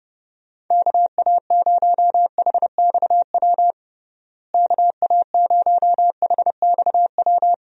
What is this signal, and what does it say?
Morse code, heard as a keyed tone, about 32 wpm, 700 Hz. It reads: KA0HXW KA0HXW
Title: USA Call Signs - 6 Characters - Repeated 2x - 30wpm